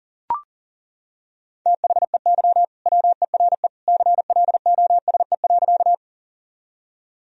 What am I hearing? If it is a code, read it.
Morse code, heard as a keyed tone, about 40 wpm, 700 Hz. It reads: THEY WERE CLOSE.
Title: Sentences from Top 1000 Words - Review Part 1 - 40wpm